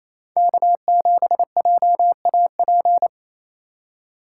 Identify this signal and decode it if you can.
Morse code, heard as a keyed tone, about 36 wpm, 700 Hz. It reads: K7JAP